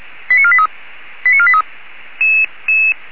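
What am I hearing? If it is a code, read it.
Five-tone selective call (ZVEI), heard as quick runs of five tones, tones of about 70 ms. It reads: 89483 89483
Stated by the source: ZVEI five-tone selective call received with an SDR.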